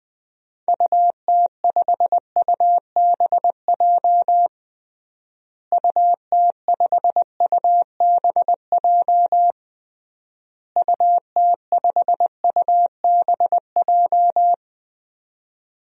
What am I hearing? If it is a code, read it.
Morse code, heard as a keyed tone, about 20 wpm, 700 Hz. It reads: UT5UBJ UT5UBJ UT5UBJ
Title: DXCC-100 Call Signs - 6 Characters - Repeated 3x - 20wpm